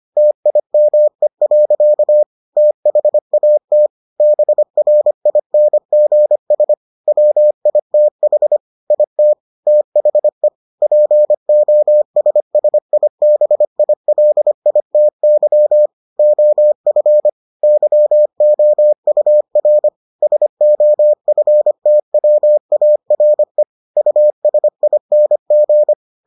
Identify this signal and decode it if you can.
Morse code, heard as a keyed tone, about 25 wpm, 600 Hz. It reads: TIME. THAT BRINGS WITH IT THE POSSIBILITY OF YOUR SOFTWARE USING